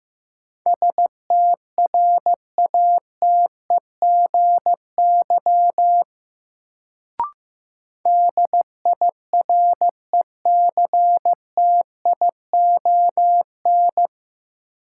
Morse code, heard as a keyed tone, about 15 wpm, 700 Hz. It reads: STRATEGY DIRECTION
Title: Sentences from Top 1000 Words - Review Part 5 - New Word Drill - 15wpm